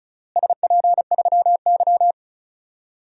Morse code, heard as a keyed tone, about 36 wpm, 700 Hz. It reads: SP3Y
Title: DXCC-100 Call Signs - 4 Characters - Repeated 3x - 35wpm